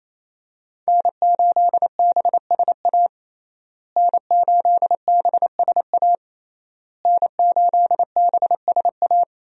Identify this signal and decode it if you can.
Morse code, heard as a keyed tone, about 28 wpm, 700 Hz. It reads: N8BSA N8BSA N8BSA
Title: USA Call Signs - 5 Characters - Repeated 3x - 28wpm